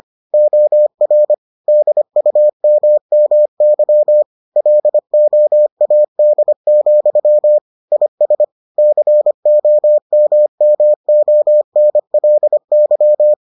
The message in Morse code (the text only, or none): OR DUMMY LOAD, IS COMMONLY